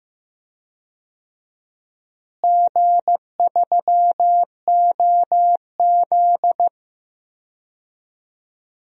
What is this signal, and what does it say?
Morse code, heard as a keyed tone, about 15 wpm, 700 Hz. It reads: G3OZ
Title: DXCC-100 Call Signs - 4 Characters - Repeated 2x - 15wpm